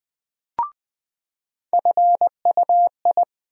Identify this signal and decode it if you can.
Morse code, heard as a keyed tone, about 20 wpm, 700 Hz. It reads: FUI
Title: USA Call Sign Postfixes - 3 Letters 20wpm